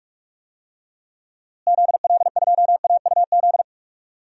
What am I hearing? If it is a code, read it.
Morse code, heard as a keyed tone, about 43 wpm, 700 Hz. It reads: ZL2AUZ
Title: DXCC-100 Call Signs - 6 Characters - Repeated 3x - 45wpm